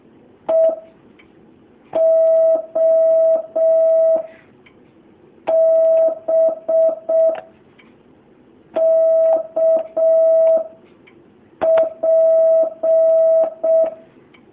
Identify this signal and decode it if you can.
Morse code, heard as a keyed tone, about 6 wpm, spaced as for 4 wpm, 650 Hz. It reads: EOB KP
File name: jh8cbh / 2019cw